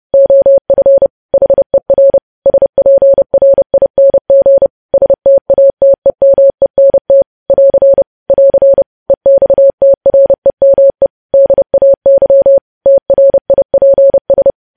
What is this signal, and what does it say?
Morse code, heard as a keyed tone, about 30 wpm, 550 Hz. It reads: OF HER SPRING STATEMENT <AR> <AR> EXTREME DAY TRIPS
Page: news headlines in morse code at 30 wpm